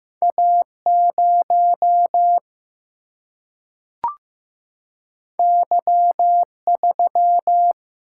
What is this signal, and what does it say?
Morse code, heard as a keyed tone, about 15 wpm, 700 Hz. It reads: A0 Y3
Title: Single-Letter Single-Number 15wpm